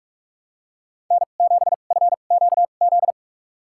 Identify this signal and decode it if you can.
Morse code, heard as a keyed tone, about 46 wpm, spaced as for 33 wpm, 700 Hz. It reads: N7FQZ
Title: USA Call Signs - 5 Characters - 2x Character Spacing - 45wpm